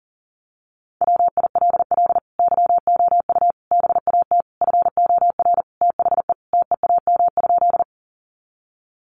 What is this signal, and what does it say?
Morse code, heard as a keyed tone, about 40 wpm, 700 Hz. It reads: WILL YOU BAT FOR THE TEAM?
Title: Sentences from Top 1500 Words 40wpm